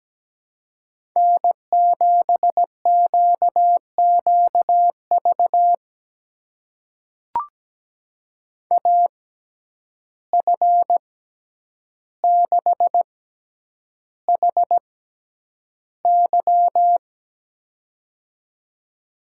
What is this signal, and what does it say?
Morse code, heard as a keyed tone, about 17 wpm, spaced as for 9 wpm, 700 Hz. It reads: N7QQV AF6HY
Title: USA Call Signs - 5 Characters - 6x Character Spacing - 17wpm